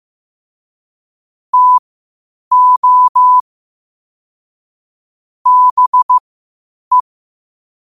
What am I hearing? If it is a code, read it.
Morse code, heard as a keyed tone, about 15 wpm, spaced as for 8 wpm, 1000 Hz. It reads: TO BE
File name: qotdfiles / 8wpm